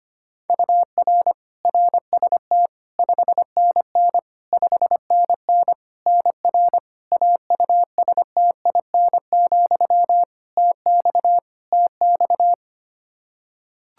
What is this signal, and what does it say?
Morse code, heard as a keyed tone, about 25 wpm, 700 Hz. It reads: UR RST 5NN 5NN NR AUSTIN, TX TX
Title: Practice QSO - Exchange 1 - Beginning - 25wpm